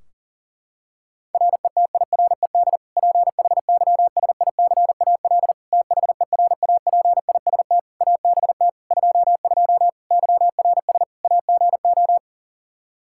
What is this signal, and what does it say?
Morse code, heard as a keyed tone, about 37 wpm, 700 Hz. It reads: RETIRED PHYSICAL THERAPIST ABT 22 YRS AGO
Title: Practice QSO - Exchange 3 - More Info - 40wpm